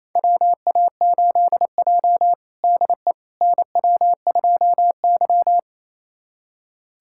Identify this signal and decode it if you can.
Morse code, heard as a keyed tone, about 28 wpm, 700 Hz. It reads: WA8J DE NW2Y
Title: USA Callsign DE Callsign - 4 Characters - 28wpm